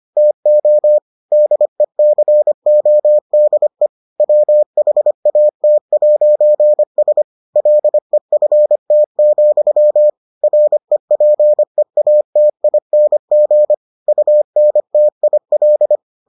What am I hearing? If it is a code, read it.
Morse code, heard as a keyed tone, about 25 wpm, 600 Hz. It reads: TO DECODE WHAT'S LEFT, REPEATING UNTIL